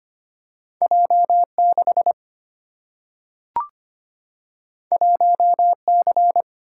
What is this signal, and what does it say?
Morse code, heard as a keyed tone, about 25 wpm, 690 Hz. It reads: J6 1C
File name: Two Letter-Number Combinations - 25wpm